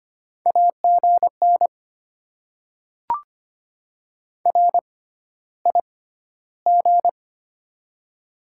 Morse code, heard as a keyed tone, about 25 wpm, spaced as for 13 wpm, 700 Hz. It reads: AGN RIG
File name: Top 20 QSO Elements - 6x Character Spacing - 25wpm